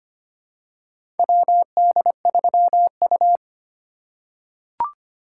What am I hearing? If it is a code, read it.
Morse code, heard as a keyed tone, about 25 wpm, 700 Hz. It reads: WD3U
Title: USA Call Signs - 4 Characters - 3x Character Spacing - 25wpm